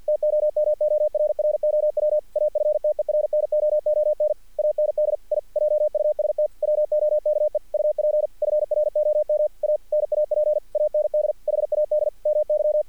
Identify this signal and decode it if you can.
Morse code, heard as a keyed tone, about 50 wpm, 600 Hz. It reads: TOMORROW AFTERNOON AND I JUST WOKE UP FROM A NAP AND HAD MY
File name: archived / CRSnet